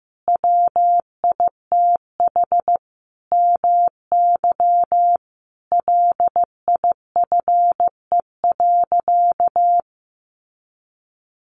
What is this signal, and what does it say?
Morse code, heard as a keyed tone, about 15 wpm, 700 Hz. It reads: WITH MY LIFE.
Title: Sentences from Top 1500 Words 15wpm